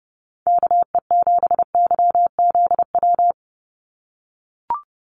Morse code, heard as a keyed tone, about 30 wpm, 700 Hz. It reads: KE7YZW
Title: USA Call Signs 30wpm